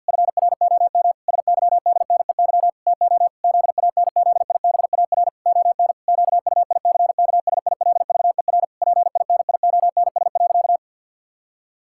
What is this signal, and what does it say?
Morse code, heard as a keyed tone, about 50 wpm, 700 Hz. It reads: FROM SYDNEY TO ZANZIBAR ON QUICKSILVER PINIONS.
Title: Pangram Challenge v2 - 50wpm